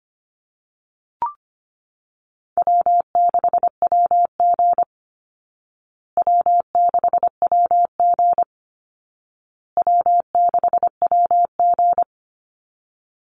Morse code, heard as a keyed tone, about 25 wpm, 700 Hz. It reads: W6WG W6WG W6WG
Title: Easy Call Signs Repeated 3x - 25wpm